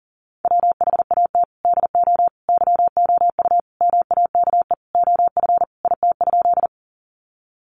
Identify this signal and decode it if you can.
Morse code, heard as a keyed tone, about 40 wpm, 700 Hz. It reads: WHAT DO YOU MAKE OF IT?
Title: Sentences from Top 100 Words 40wpm